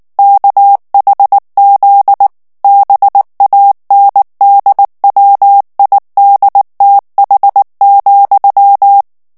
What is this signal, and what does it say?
Morse code, heard as a keyed tone, about 19 wpm, 800 Hz. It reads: KHZ BANDWIDTH,
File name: ANGLAIS texte clair 3 vitesse 18 mots minute